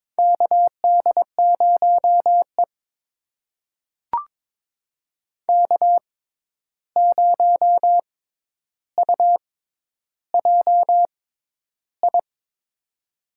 Morse code, heard as a keyed tone, about 22 wpm, spaced as for 10 wpm, 700 Hz. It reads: KD0E K0UJI
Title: USA Call Signs - 6x Character Spacing - 22wpm